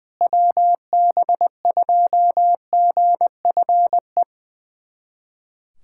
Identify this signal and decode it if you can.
Morse code, heard as a keyed tone, about 20 wpm, 700 Hz. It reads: WB2GFE
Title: USA Callsign DE Callsign 20wpm